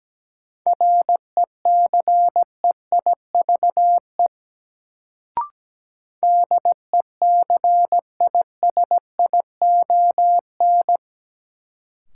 Morse code, heard as a keyed tone, about 17 wpm, 700 Hz. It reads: RECEIVE DECISION
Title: Sentences from Top 1000 Words - New Word Drill - 17wpm